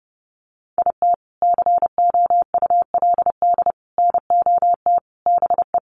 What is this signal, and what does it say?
Morse code, heard as a keyed tone, about 30 wpm, 700 Hz. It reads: IT COULD NOT BE
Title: Sentences from Top 1200 Words 30wpm